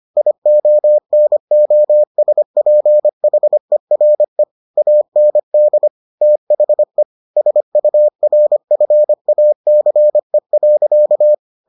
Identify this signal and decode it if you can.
Morse code, heard as a keyed tone, about 25 wpm, 600 Hz. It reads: IONOSPHERE AND THE SURFACE.